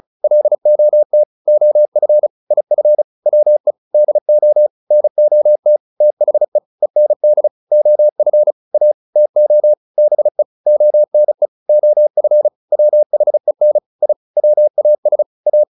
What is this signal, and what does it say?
Morse code, heard as a keyed tone, about 35 wpm, 600 Hz. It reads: LOT OF IF WE DO NOT THE END OF A TO BE ONE OF WHEN I WAS A